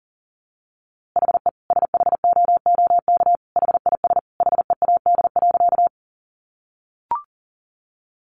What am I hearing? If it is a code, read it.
Morse code, heard as a keyed tone, about 40 wpm, 700 Hz. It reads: HE SHOOK HIS HEAD.
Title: A Sound Of Thunder 40wpm